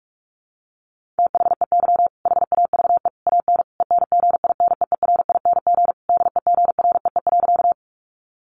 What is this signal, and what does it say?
Morse code, heard as a keyed tone, about 44 wpm, 700 Hz. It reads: THEY HAVE AN ENGINEERING DEGREE.
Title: Sentences from Top 1500 Words 45wpm